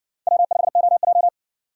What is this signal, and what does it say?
Morse code, heard as a keyed tone, about 54 wpm, 700 Hz. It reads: W5OJ